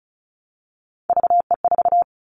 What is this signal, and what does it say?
Morse code, heard as a keyed tone, about 34 wpm, 700 Hz. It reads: VE4
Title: Callsign Prefixes for Top 25 DXCC - Repeated 3x - 35wpm